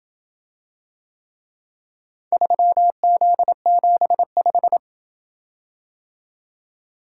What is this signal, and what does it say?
Morse code, heard as a keyed tone, about 28 wpm, 700 Hz. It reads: 3Z75